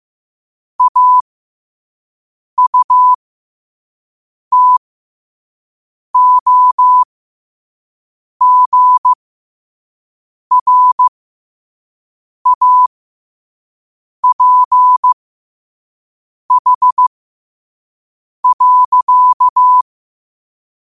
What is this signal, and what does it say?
Morse code, heard as a keyed tone, about 15 wpm, spaced as for 5 wpm, 1000 Hz. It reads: AUTOGRAPH.
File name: qotdfiles / 5wpm